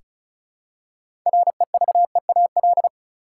Morse code, heard as a keyed tone, about 36 wpm, 700 Hz. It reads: REVEAL
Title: Six Letter Words - 35wpm